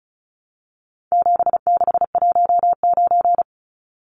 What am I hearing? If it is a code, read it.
Morse code, heard as a keyed tone, about 34 wpm, 700 Hz. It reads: 7619
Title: Numbers - 4-Digit 35wpm